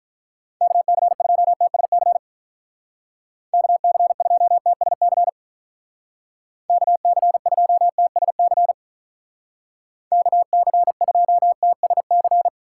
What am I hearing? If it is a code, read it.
Morse code, heard as a keyed tone, about 43 wpm, 700 Hz. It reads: KC2TSC KC2TSC KC2TSC KC2TSC
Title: Speed Racing - USA Call Signs 35wpm